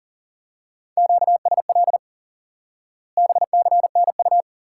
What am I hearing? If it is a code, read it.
Morse code, heard as a keyed tone, about 40 wpm, 700 Hz. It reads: QSL BCNU
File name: IQE Course 3 - Lesson 89 - YAGI - Two Elements - 40wpm